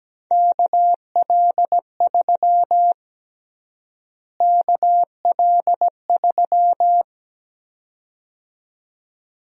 Morse code, heard as a keyed tone, about 17 wpm, 700 Hz. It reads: KL3 KL3
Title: USA Call Sign Prefixes - Repeated 2x - 17wpm